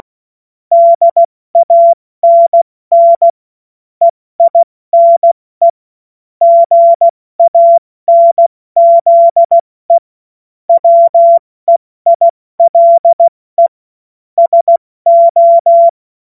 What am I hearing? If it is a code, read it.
Morse code, heard as a keyed tone, about 16 wpm, 680 Hz. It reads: DANN EINE GANZE WEILE SO
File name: Morsekurs / Witze / Witz 02 Januar 2024 70 F80